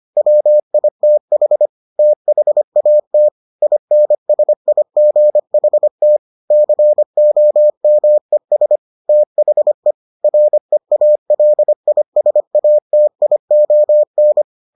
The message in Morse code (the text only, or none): WITH THAT INSIGHT COMES THE REALISATION